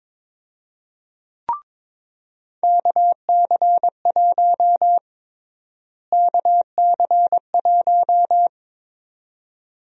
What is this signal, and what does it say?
Morse code, heard as a keyed tone, about 22 wpm, 700 Hz. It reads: KC1 KC1
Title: USA Call Sign Prefixes - Repeated 2x - 22wpm